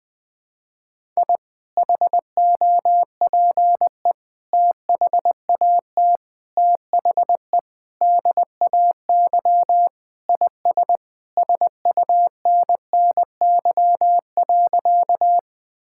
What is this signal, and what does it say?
Morse code, heard as a keyed tone, about 21 wpm, 700 Hz. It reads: I HOPE THAT THE DAY IS SUNNY.
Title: Sentences with 2-Word Phrases - 20wpm